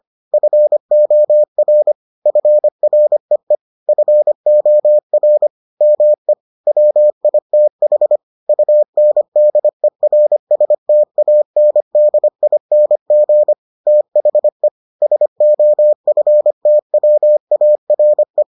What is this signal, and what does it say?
Morse code, heard as a keyed tone, about 25 wpm, 600 Hz. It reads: FOR FREE FOR ME WITH UNDERSTANDING THE SOFTWARE